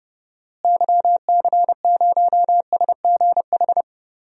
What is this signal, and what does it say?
Morse code, heard as a keyed tone, about 30 wpm, 700 Hz. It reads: YC0SGH
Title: DXCC-100 Call Signs - 6 Characters - Repeated 3x - 30wpm